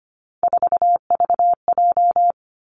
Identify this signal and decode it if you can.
Morse code, heard as a keyed tone, about 25 wpm, 700 Hz. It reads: 4VJ
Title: Three Challenging Letter-Number Combinations 25wpm